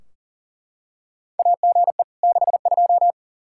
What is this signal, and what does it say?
Morse code, heard as a keyed tone, about 38 wpm, 700 Hz. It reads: AGE 62
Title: Practice QSO - Exchange 3 - More Info - 40wpm